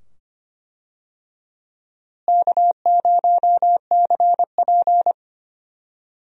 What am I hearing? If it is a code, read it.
Morse code, heard as a keyed tone, about 25 wpm, 700 Hz. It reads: K0CP
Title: USA Call Signs - 4 Characters - Repeated 2x - 25wpm